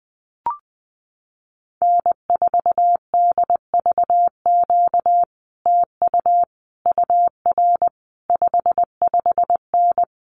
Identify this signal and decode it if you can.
Morse code, heard as a keyed tone, about 20 wpm, 700 Hz. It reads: N4DVQ TU UR 55N
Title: Common QSO Phrases Encore 20wpm